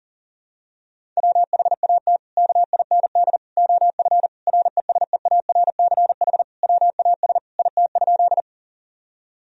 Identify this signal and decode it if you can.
Morse code, heard as a keyed tone, about 40 wpm, 700 Hz. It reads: WHAT KIND OF RESEARCH WAS IT?
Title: Sentences from Top 1000 Words - Review Part 4 - 40wpm